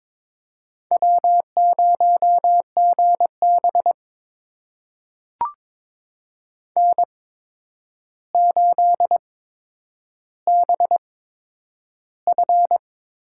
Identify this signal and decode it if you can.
Morse code, heard as a keyed tone, about 22 wpm, spaced as for 9 wpm, 700 Hz. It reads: W0GB N8BF
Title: USA Call Signs - 4 Characters - 8x Character Spacing - 22wpm